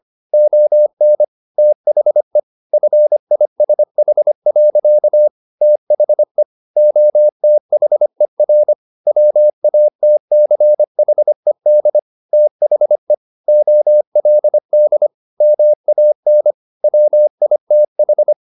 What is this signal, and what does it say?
Morse code, heard as a keyed tone, about 25 wpm, 600 Hz. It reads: ON THE FISH. THE OTHER WATCHED THE OLD MAN WITH